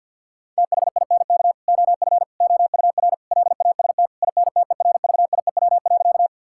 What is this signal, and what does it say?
Morse code, heard as a keyed tone, about 50 wpm, 700 Hz. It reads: THINK OF OUR LAST INTERVIEW.
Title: Sentences from Top 1600 Words 50wpm